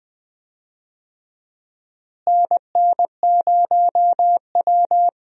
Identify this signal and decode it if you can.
Morse code, heard as a keyed tone, about 20 wpm, 700 Hz. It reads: NN0W
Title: USA Call Signs - 4 Characters - Repeated 3x - 20wpm